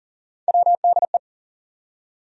Morse code, heard as a keyed tone, about 40 wpm, 700 Hz. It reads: WDE
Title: USA Call Sign Postfixes - 3 Letters 40wpm